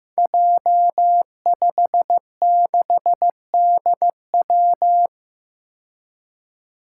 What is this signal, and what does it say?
Morse code, heard as a keyed tone, about 15 wpm, 700 Hz. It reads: J56DW
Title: Five Challenging Letter-Number Combinations 15wpm